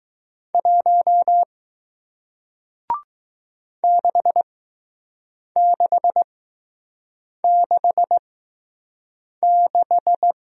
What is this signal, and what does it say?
Morse code, heard as a keyed tone, about 20 wpm, spaced as for 6 wpm, 700 Hz. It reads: 1 6666